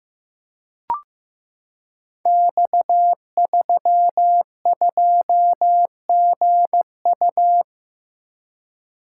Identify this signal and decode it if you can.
Morse code, heard as a keyed tone, about 15 wpm, 700 Hz. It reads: X32GU
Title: Code Groups - 5 Letter:Numbers - 15wpm